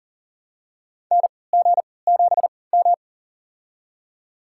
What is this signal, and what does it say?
Morse code, heard as a keyed tone, about 40 wpm, spaced as for 23 wpm, 700 Hz. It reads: NG7M
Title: USA Call Signs - 4 Characters - 3x Character Spacing - 40wpm